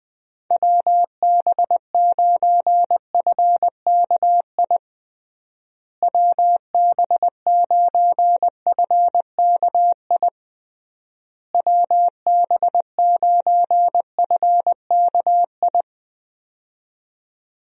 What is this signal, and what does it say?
Morse code, heard as a keyed tone, about 21 wpm, 700 Hz. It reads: WB9FKI WB9FKI WB9FKI
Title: USA Call Signs - 6 Characters - Repeated 3x - 20wpm